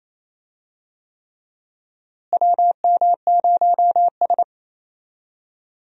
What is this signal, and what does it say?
Morse code, heard as a keyed tone, about 28 wpm, 700 Hz. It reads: WM0S
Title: USA Call Signs - 4 Characters - Repeated 3x - 28wpm